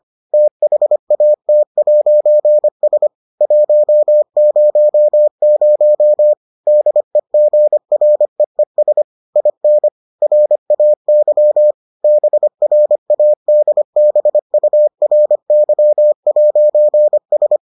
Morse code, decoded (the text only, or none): THAT'S 100 DEGREES IN RAY BRADBURY'S